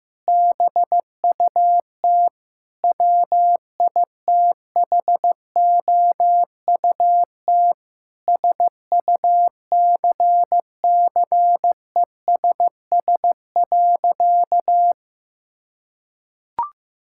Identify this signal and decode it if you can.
Morse code, heard as a keyed tone, about 15 wpm, 700 Hz. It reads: BUT WITHOUT SUCCESS.
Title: Sentences from Top 1000 Words - Review Part 6 - 15wpm